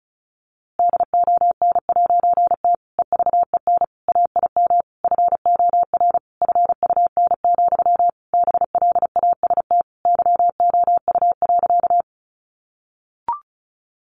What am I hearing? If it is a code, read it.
Morse code, heard as a keyed tone, about 34 wpm, 700 Hz. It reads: DON'T EVEN AIM FOR FUN, BLAST YOU.
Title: A Sound Of Thunder 35wpm